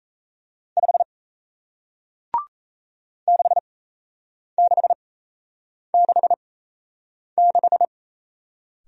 Morse code, heard as a keyed tone, about 37 wpm, spaced as for 8 wpm, 700 Hz. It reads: 5 6666